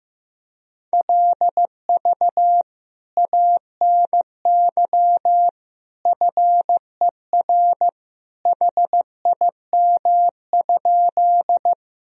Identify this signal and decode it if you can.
Morse code, heard as a keyed tone, about 14 wpm, 700 Hz. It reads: LV ANY FER HIM?
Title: Sentences from Top 400 Words and Abbreviations 15wpm